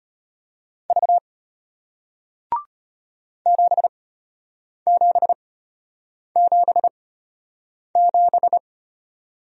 Morse code, heard as a keyed tone, about 33 wpm, spaced as for 8 wpm, 700 Hz. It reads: V 7777